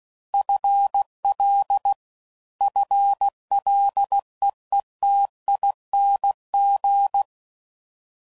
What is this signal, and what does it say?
Morse code, heard as a keyed tone, about 16 wpm, 800 Hz. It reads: FL FLEETING
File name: Prefixes / FLs & THs / FL Prefixes 3